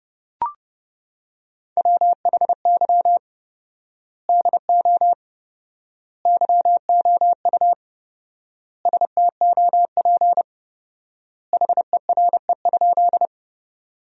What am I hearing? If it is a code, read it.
Morse code, heard as a keyed tone, about 30 wpm, 700 Hz. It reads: WHY DO YOU STOP HERE?
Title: 4x Word-Spacing - Sentences from Top 300 Words - 30wpm